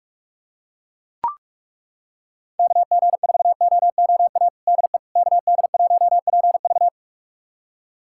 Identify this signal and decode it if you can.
Morse code, heard as a keyed tone, about 41 wpm, 700 Hz. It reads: KG4OOA DE KD1PV
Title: USA Callsign DE Callsign 45wpm